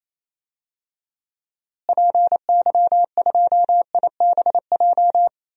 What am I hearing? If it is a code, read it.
Morse code, heard as a keyed tone, about 28 wpm, 700 Hz. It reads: PY2IBJ
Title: DXCC-100 Call Signs - 6 Characters - Repeated 3x - 28wpm